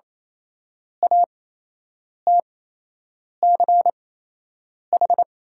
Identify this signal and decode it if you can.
Morse code, heard as a keyed tone, about 28 wpm, spaced as for 8 wpm, 690 Hz. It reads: ATCH
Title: 8x Character Spacing - Top 300 Words - 28wpm